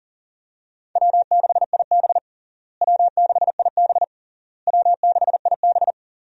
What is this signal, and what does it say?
Morse code, heard as a keyed tone, about 40 wpm, 700 Hz. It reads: W6IB W6IB W6IB
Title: USA Call Signs - 4 Characters - Repeated 3x - 40wpm